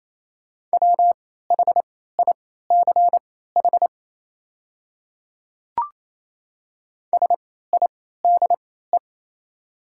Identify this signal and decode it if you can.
Morse code, heard as a keyed tone, about 28 wpm, spaced as for 16 wpm, 700 Hz. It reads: WHICH SIDE
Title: 3x Character Spacing - Top 100 Words - 28wpm